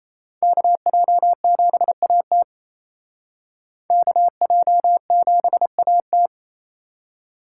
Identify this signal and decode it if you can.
Morse code, heard as a keyed tone, about 32 wpm, 700 Hz. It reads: KJ7AT KJ7AT
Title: Speed Racing - USA 2x2 Call Signs 28wpm